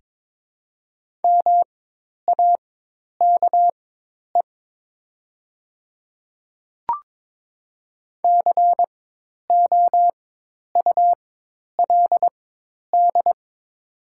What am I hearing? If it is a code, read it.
Morse code, heard as a keyed tone, about 21 wpm, spaced as for 10 wpm, 700 Hz. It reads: MAKE COULD
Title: 4x Character Spacing - Top 100 Words - 22wpm